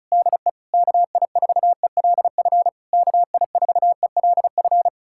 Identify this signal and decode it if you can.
Morse code, heard as a keyed tone, about 40 wpm, 700 Hz. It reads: DE KI4ELF KI4ELF